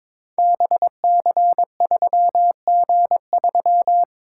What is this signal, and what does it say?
Morse code, heard as a keyed tone, about 22 wpm, 690 Hz. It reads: BC3G3
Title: Code Groups - 5 Letter:Numbers - 22wpm